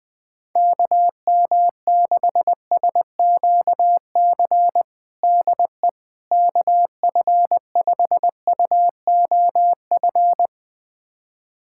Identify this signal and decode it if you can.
Morse code, heard as a keyed tone, about 20 wpm, 700 Hz. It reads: KM6SQC DE KF5UOF
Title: USA Callsign DE Callsign - 6 Characters - 20wpm